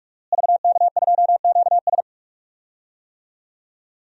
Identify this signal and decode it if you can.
Morse code, heard as a keyed tone, about 46 wpm, 700 Hz. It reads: VK2QS
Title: DXCC-100 Call Signs - 5 Characters - Repeated 2x - 45wpm